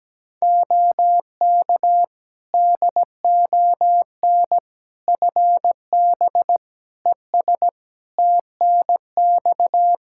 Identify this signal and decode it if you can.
Morse code, heard as a keyed tone, about 17 wpm, 700 Hz. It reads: OK DON FB ES TNX
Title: Practice QSO - Exchange 2 - Info - 17wpm